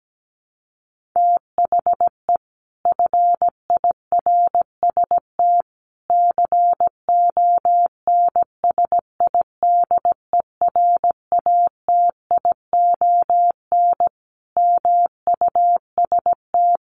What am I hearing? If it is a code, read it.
Morse code, heard as a keyed tone, about 17 wpm, 700 Hz. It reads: THE FIRST CONSIDERATION MUST
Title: Sentences from Top 1500 Words 17wpm